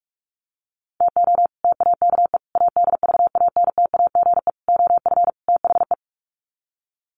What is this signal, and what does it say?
Morse code, heard as a keyed tone, about 44 wpm, 700 Hz. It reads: TO TAKE ADVANTAGE OF THE
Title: Sets of 5 Words 45wpm